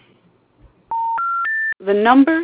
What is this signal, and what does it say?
Special information tone, with speech.